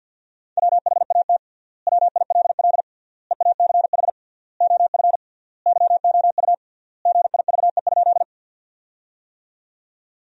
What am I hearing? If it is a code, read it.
Morse code, heard as a keyed tone, about 53 wpm, 700 Hz. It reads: WHAT WILL EACH OF YOU GIVE?
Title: 3x Word-Spacing - Sentences from Top 200 Words - 50wpm